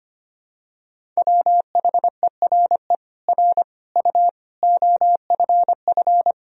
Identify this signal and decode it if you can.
Morse code, heard as a keyed tone, about 25 wpm, 700 Hz. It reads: WHERE R U OFF
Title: Sentences from Top 200 Words and Abbreviations 25wpm